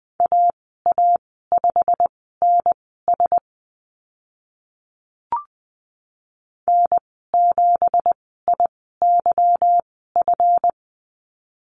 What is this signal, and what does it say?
Morse code, heard as a keyed tone, about 20 wpm, spaced as for 14 wpm, 700 Hz. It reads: AA5NS N7IYF
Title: 167 Lesson 27g - Number 7 - Call Signs 20wpm